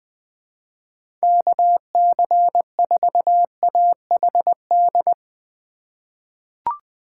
Morse code, heard as a keyed tone, about 20 wpm, 700 Hz. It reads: KC4AHD
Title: USA Call Signs - 6 Characters - 4x Character Spacing - 20wpm